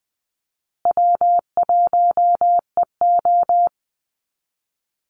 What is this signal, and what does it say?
Morse code, heard as a keyed tone, about 20 wpm, 700 Hz. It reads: W1EO
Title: USA Call Signs Repeated 3x 20wpm